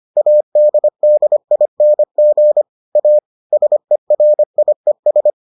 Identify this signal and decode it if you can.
Morse code, heard as a keyed tone, about 25 wpm, 600 Hz. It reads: ADDING A SERIES